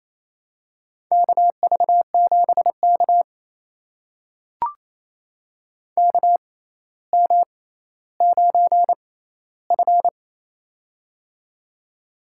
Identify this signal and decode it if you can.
Morse code, heard as a keyed tone, about 28 wpm, spaced as for 14 wpm, 700 Hz. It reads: KV7K KM9F
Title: USA Call Signs - 4 Characters - 6x Character Spacing - 28wpm